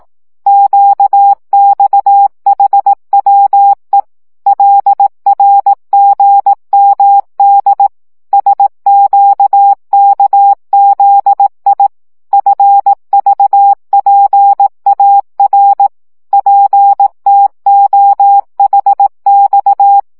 Morse code, heard as a keyed tone, about 18 wpm, 800 Hz. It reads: QXHWE LRGMD SQKZI FVPAR PTOHX
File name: morze vizsga 2011-1 15